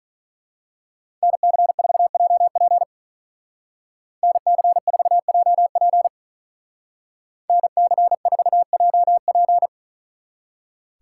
Morse code, heard as a keyed tone, about 41 wpm, 700 Hz. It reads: NC4JP NC4JP NC4JP
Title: Speed Racing - USA 2x2 Call Signs 35wpm